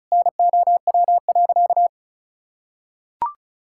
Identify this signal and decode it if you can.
Morse code, heard as a keyed tone, about 34 wpm, 690 Hz. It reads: NOW.
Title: Sentences from Top 1000 Words - Review Part 6 - 35wpm